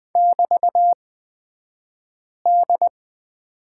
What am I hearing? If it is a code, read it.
Morse code, heard as a keyed tone, about 20 wpm, spaced as for 5 wpm, 700 Hz. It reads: <BT>D
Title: 223 Lesson 36d - BT - Two Characters 20wpm